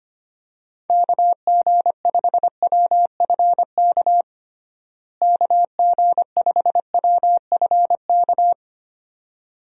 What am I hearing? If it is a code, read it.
Morse code, heard as a keyed tone, about 25 wpm, 700 Hz. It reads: KG5WFK KG5WFK
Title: USA Call Signs - 6 Characters - Repeated 2x - 25wpm